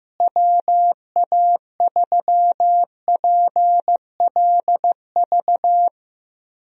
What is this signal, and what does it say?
Morse code, heard as a keyed tone, about 15 wpm, 700 Hz. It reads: WA3PLV
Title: USA Call Signs - 6 Characters - Repeated 2x - 15wpm